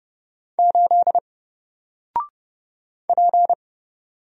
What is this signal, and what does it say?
Morse code, heard as a keyed tone, about 30 wpm, 700 Hz. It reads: 8 P
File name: Single-Letter-Number-30wpm